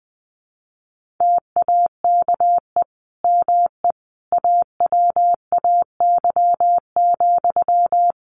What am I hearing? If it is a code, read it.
Morse code, heard as a keyed tone, about 20 wpm, 700 Hz. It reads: TAKE ME AWAY,
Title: Sentences from Top 1400 Words 20wpm